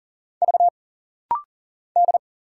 Morse code, heard as a keyed tone, about 40 wpm, 700 Hz. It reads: V D
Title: Single Letter-Number Alt - Rapid-Fire 40wpm